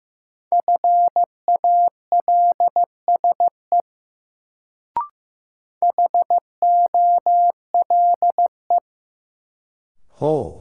Morse code, heard as a keyed tone, about 15 wpm, 700 Hz. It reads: FALSE HOLE
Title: Sentences from Top 1600 Words - New Word Drill - 15wpm